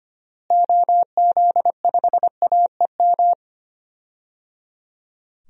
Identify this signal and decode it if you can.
Morse code, heard as a keyed tone, about 25 wpm, 700 Hz. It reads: OZ5AEM
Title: DXCC-100 Call Signs - 6 Characters - Repeated 3x - 25wpm